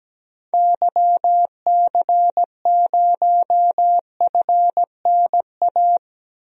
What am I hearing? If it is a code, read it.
Morse code, heard as a keyed tone, about 17 wpm, 700 Hz. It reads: YC0FNA
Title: DXCC-100 Call Signs - 6 Characters - 17wpm